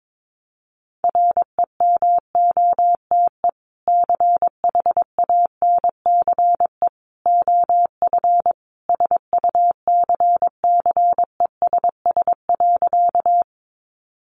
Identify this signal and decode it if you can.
Morse code, heard as a keyed tone, about 22 wpm, 700 Hz. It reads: REMOTE CHANCE OF SUCCESS.
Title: Sentences from Top 1300 Words 22wpm